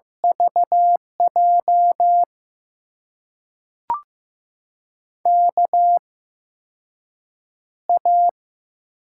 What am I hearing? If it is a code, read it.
Morse code, heard as a keyed tone, about 15 wpm, spaced as for 6 wpm, 700 Hz. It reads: VJ KA
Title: USA Call Signs - 6 Characters - 8x Character Spacing - 15wpm